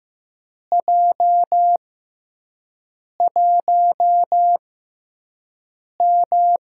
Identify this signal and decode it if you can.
Morse code, heard as a keyed tone, about 15 wpm, spaced as for 5 wpm, 700 Hz. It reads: J1M
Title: USA Call Signs - 4 Characters - 6x Character Spacing - 15wpm